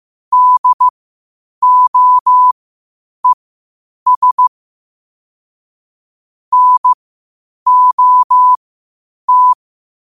Morse code, heard as a keyed tone, about 15 wpm, spaced as for 8 wpm, 1000 Hz. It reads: DOES NOT
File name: qotdfiles / 8wpm